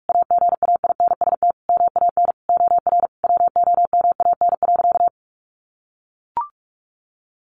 Morse code, heard as a keyed tone, about 44 wpm, 700 Hz. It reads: AGAINST MAN OR WOMAN.
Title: Sentences from Top 1400 Words 45wpm